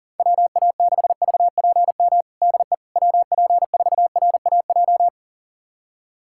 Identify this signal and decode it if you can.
Morse code, heard as a keyed tone, about 40 wpm, 700 Hz. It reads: WA6VPM DE WP4RAJ
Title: USA Callsign DE Callsign - 6 Characters - 40wpm